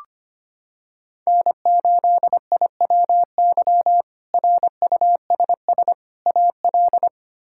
Morse code, heard as a keyed tone, about 25 wpm, 700 Hz. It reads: N8IWY RUSS AL
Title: K1USN SST QSO Practice - 25wpm